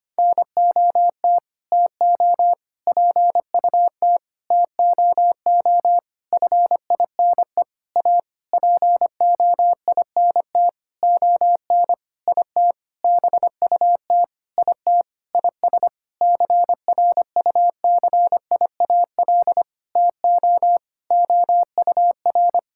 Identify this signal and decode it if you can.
Morse code, heard as a keyed tone, about 25 wpm, 700 Hz. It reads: NOT TO PUT TOO FINE A POINT ON IT BUT IT IS CRUCIAL TO OUR
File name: practice / 25wpm / Sentences with 9-Word Phrases - 25wpm